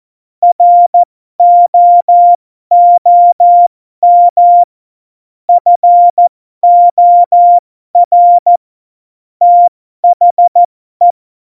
Morse code, heard as a keyed tone, about 14 wpm, spaced as for 12 wpm, 700 Hz. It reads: ROOM FOR THE
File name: Morsekurs / Witze